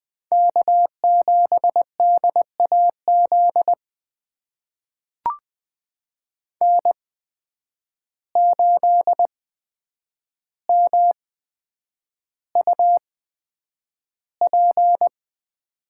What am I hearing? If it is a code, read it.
Morse code, heard as a keyed tone, about 20 wpm, spaced as for 9 wpm, 700 Hz. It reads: K7DAZ N8MUP
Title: USA Call Signs - 5 Characters - 8x Character Spacing - 20wpm